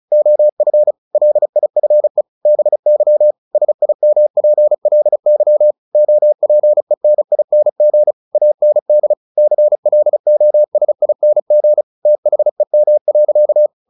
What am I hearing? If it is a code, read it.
Morse code, heard as a keyed tone, about 34 wpm, 600 Hz. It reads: OF LIFE BY SIMPLY OPENING AND CLOSING THEM.
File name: the old man and the sea 2 35wpm